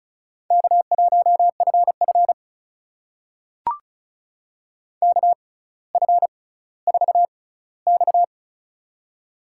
Morse code, heard as a keyed tone, about 36 wpm, spaced as for 18 wpm, 700 Hz. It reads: K1FF KF4X
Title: USA Call Signs - 4 Characters - 6x Character Spacing - 35wpm